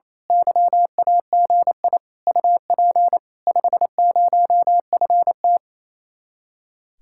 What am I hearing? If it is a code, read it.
Morse code, heard as a keyed tone, about 26 wpm, 700 Hz. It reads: YAGI UP 50FT
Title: Speed-Racing Common QSO Phrases - 28wpm